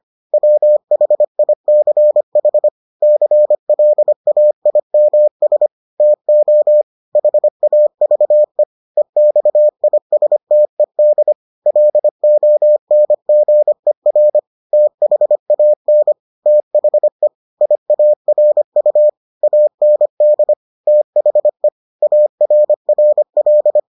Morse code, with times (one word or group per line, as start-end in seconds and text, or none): 0.3-2.7 WHICH
3.0-5.7 CLAIMS
6.0-6.8 TO
7.2-8.6 HAVE
9.0-11.3 EXISTED
11.7-14.4 LONGER
14.7-16.1 THAN
16.5-17.3 THE
17.6-19.1 IARU
19.4-20.5 AND
20.9-21.7 THE
22.0-23.8 ARRL